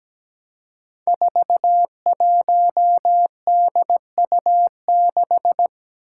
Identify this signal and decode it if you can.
Morse code, heard as a keyed tone, about 17 wpm, 700 Hz. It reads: 41DU6
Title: Five Challenging Letter-Number Combinations 17wpm